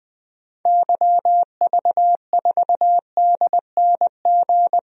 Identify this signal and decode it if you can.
Morse code, heard as a keyed tone, about 20 wpm, 700 Hz. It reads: YV4DNG
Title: DXCC-100 Call Signs - 6 Characters - 20wpm